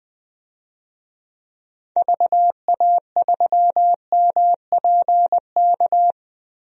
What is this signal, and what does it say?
Morse code, heard as a keyed tone, about 20 wpm, 700 Hz. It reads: VA3MPK
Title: DXCC-100 Call Signs - 6 Characters - Repeated 2x - 20wpm